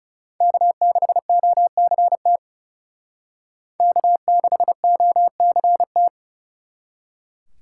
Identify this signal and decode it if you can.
Morse code, heard as a keyed tone, about 38 wpm, 700 Hz. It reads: K6OCT K6OCT